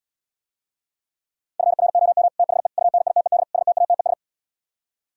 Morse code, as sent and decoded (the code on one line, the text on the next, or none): ..-. .-. --- -- / - .... / ..- -. .. - . -.. / ... - .- - . ...
FROM THE UNITED STATES